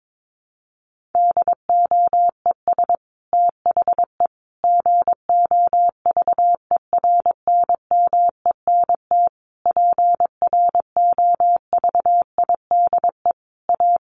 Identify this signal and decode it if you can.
Morse code, heard as a keyed tone, about 22 wpm, 700 Hz. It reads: DOES THE GOVERNMENT PROVIDE A
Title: Sentences from Top 1500 Words 22wpm